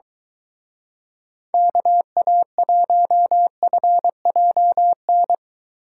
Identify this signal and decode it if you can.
Morse code, heard as a keyed tone, about 22 wpm, 700 Hz. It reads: KA1FJN